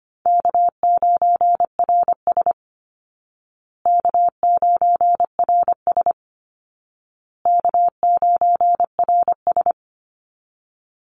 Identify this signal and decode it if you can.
Morse code, heard as a keyed tone, about 25 wpm, 700 Hz. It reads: K9RS K9RS K9RS
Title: Call Signs Repeated 2x - 25wpm